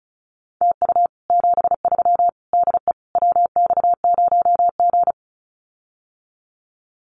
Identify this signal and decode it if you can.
Morse code, heard as a keyed tone, about 34 wpm, 700 Hz. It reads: TU 73 DE WX0G
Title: Common QSO Phrases Encore 35wpm